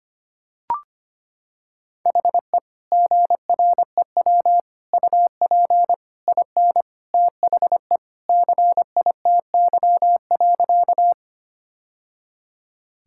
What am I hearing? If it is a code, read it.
Morse code, heard as a keyed tone, about 25 wpm, 700 Hz. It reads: HE GREW UP IN THE CITY.
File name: practice / 25wpm / Sentences with 3-Word Phrases - 25wpm